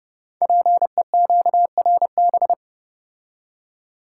Morse code, heard as a keyed tone, about 28 wpm, 700 Hz. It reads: PEQRB
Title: Code Groups - 5 Letters - 30wpm